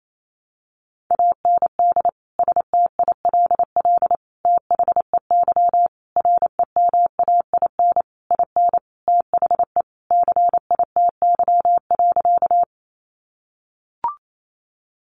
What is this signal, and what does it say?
Morse code, heard as a keyed tone, about 28 wpm, 700 Hz. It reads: AND STILL THEY REMAIN IN THE CITY.
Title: Sentences from Top 1200 Words 28wpm